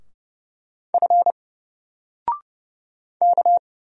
Morse code, heard as a keyed tone, about 31 wpm, 700 Hz. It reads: F K
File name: Single-Letter-Number-30wpm